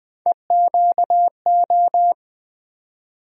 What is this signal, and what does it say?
Morse code, heard as a keyed tone, about 20 wpm, 700 Hz. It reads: EQO
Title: USA Call Sign Postfixes - 3 Letters 20wpm